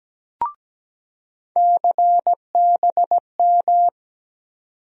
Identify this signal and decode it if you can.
Morse code, heard as a keyed tone, about 17 wpm, 700 Hz. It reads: CBM
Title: USA Call Sign Postfixes - 3 Letters 17wpm